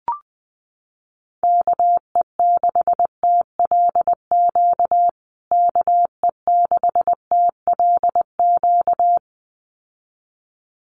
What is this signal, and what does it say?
Morse code, heard as a keyed tone, about 20 wpm, 700 Hz. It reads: KE6TLQ KE6TLQ
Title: USA Call Signs Repeated 2x 20wpm